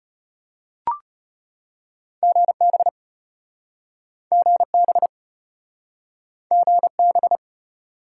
Morse code, heard as a keyed tone, about 33 wpm, 700 Hz. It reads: GB GB GB